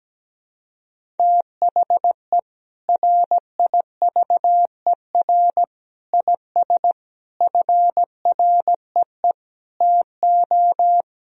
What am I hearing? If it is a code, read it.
Morse code, heard as a keyed tone, about 17 wpm, 700 Hz. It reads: THE RIVER IS FREE TO
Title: Sentences from Top 1000 Words - Review Part 1 - 17wpm